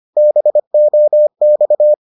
Morse code, heard as a keyed tone, about 25 wpm, 600 Hz. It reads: BOX